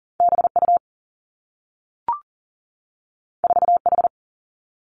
Morse code, heard as a keyed tone, about 40 wpm, 700 Hz. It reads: B U 4 H
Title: Two Challenging Letter-Number Combinations 40wpm